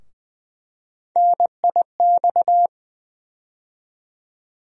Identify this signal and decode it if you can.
Morse code, heard as a keyed tone, about 20 wpm, 700 Hz. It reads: NIX
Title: USA Call Sign Postfixes - 3 Letters 20wpm